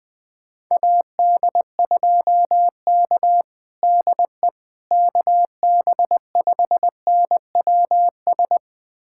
Morse code, heard as a keyed tone, about 20 wpm, 700 Hz. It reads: AD2K DE KB5NWS
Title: USA Callsign DE Callsign 20wpm